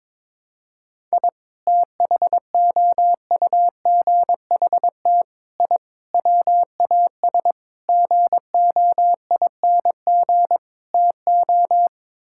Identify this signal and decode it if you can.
Morse code, heard as a keyed tone, about 22 wpm, 700 Hz. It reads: I THOUGHT I WAS GOING TO
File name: practice / 22wpm / Sets of 6 Words - 22wpm